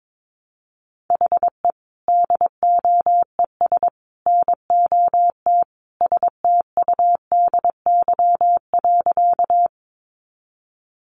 Morse code, heard as a keyed tone, about 22 wpm, 700 Hz. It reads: HE DOES NOT STUDY.
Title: Additional Easy Phrases 22wpm